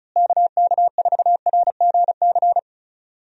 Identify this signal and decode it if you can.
Morse code, heard as a keyed tone, about 36 wpm, 700 Hz. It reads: KK4RGC
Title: USA Call Signs - 6 Characters - Repeated 3x - 35wpm